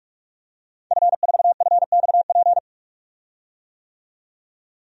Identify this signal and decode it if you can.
Morse code, heard as a keyed tone, about 46 wpm, 700 Hz. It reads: F4FXP
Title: DXCC-100 Call Signs - 5 Characters - 45wpm